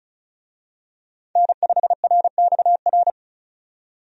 Morse code, heard as a keyed tone, about 36 wpm, 700 Hz. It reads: N5RXR
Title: USA Call Signs - 5 Characters - 35wpm